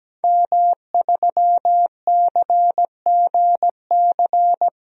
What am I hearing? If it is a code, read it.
Morse code, heard as a keyed tone, about 17 wpm, 700 Hz. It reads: M3CGC
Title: DXCC-100 Call Signs - 5 Characters - 17wpm